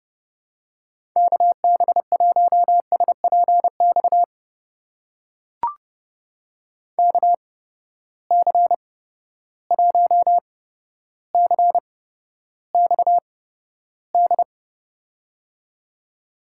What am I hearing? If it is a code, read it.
Morse code, heard as a keyed tone, about 30 wpm, spaced as for 13 wpm, 700 Hz. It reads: KB1SPX KC1CXD
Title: USA Call Signs - 6 Characters - 8x Character Spacing - 30wpm